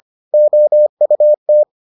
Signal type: Morse code

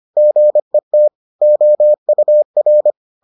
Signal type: Morse code

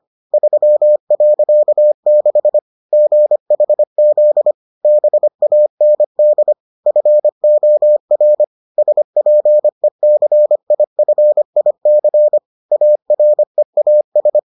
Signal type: Morse code

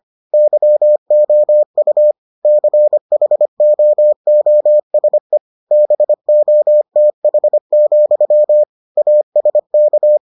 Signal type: Morse code

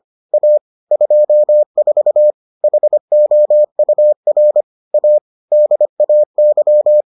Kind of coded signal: Morse code

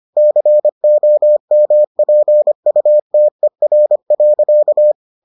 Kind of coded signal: Morse code

